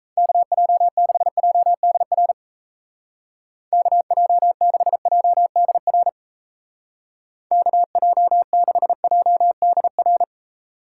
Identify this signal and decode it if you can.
Morse code, heard as a keyed tone, about 36 wpm, 700 Hz. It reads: KJ6JDR KJ6JDR KJ6JDR